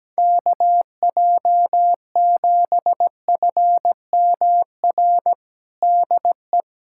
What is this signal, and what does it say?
Morse code, heard as a keyed tone, about 17 wpm, 700 Hz. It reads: KJ7FMR DE
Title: USA Callsign DE Callsign 17wpm